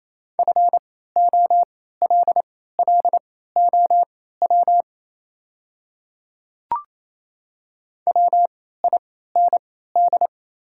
Morse code, heard as a keyed tone, about 28 wpm, spaced as for 16 wpm, 700 Hz. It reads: FOLLOW WIND
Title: 3x Character Spacing - Top 500 Words - 28wpm